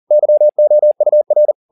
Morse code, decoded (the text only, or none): YOUR